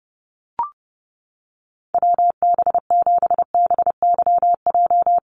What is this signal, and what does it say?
Morse code, heard as a keyed tone, about 30 wpm, 700 Hz. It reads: WB7BYJ
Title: USA Call Signs 30wpm